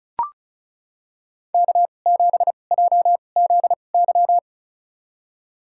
Morse code, heard as a keyed tone, about 36 wpm, spaced as for 25 wpm, 700 Hz. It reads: K7JZY
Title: USA Call Signs - 2x Character Spacing - 35wpm